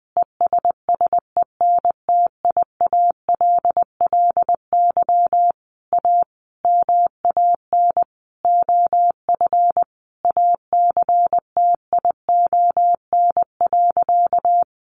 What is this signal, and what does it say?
Morse code, heard as a keyed tone, about 20 wpm, 700 Hz. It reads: ESSENTIALLY A MAN OF ACTION.
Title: Sentences from Top 1300 Words 20wpm